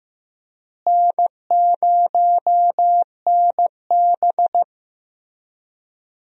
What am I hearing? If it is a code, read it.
Morse code, heard as a keyed tone, about 16 wpm, 700 Hz. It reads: N0NB
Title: USA Call Signs - 4 Characters - Repeated 3x - 15wpm